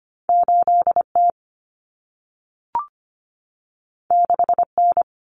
Morse code, heard as a keyed tone, about 25 wpm, 700 Hz. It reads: 8T 6N
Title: Common Cut Numbers - 2-Digit 25wpm